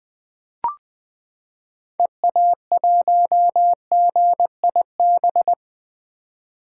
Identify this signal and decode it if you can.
Morse code, heard as a keyed tone, about 20 wpm, 700 Hz. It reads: EA1GIB
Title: 2023 POTA Hunter Call Signs - 20wpm